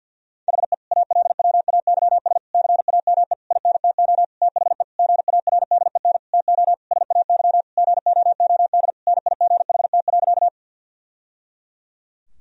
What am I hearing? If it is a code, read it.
Morse code, heard as a keyed tone, about 52 wpm, 700 Hz. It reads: HE ALWAYS CAME INTO THE GARDEN TO SAY GOOD NIGHT.
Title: Sentences from Top 1000 Words - Review Part 4 - 50wpm